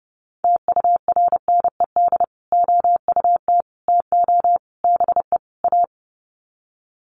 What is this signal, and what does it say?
Morse code, heard as a keyed tone, about 30 wpm, 700 Hz. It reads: TURNED OUT TO BE A
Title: Sets of 5 Words 30wpm